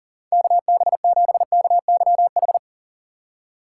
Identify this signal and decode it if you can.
Morse code, heard as a keyed tone, about 37 wpm, 700 Hz. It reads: KB7KYH
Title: USA Call Signs - 6 Characters - 6x Character Spacing - 40wpm